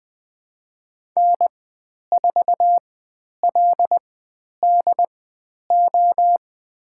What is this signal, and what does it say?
Morse code, heard as a keyed tone, about 20 wpm, spaced as for 10 wpm, 700 Hz. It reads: N4LDO
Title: 095 Lesson 16g - Letter c - Call Signs 20wpm